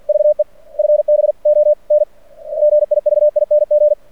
Morse code, heard as a keyed tone, about 50 wpm, 600 Hz. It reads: VE FB ON GIVING
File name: archived / CRSnet